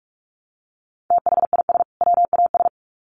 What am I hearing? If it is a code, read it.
Morse code, heard as a keyed tone, about 44 wpm, 700 Hz. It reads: THIS WAS
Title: Sets of 2 Words 45wpm